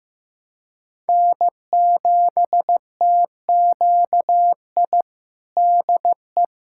Morse code, heard as a keyed tone, about 15 wpm, 700 Hz. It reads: N7TQI DE
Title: USA Callsign DE Callsign 15wpm